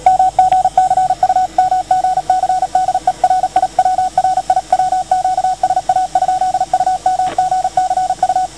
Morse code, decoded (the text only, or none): MGC UMGC DE RIW RIW QSA? UMGC U